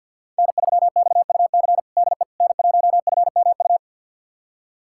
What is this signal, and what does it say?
Morse code, heard as a keyed tone, about 52 wpm, 700 Hz. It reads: N3XUC DE N1FMU
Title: USA Callsign DE Callsign - 5 Characters - 50wpm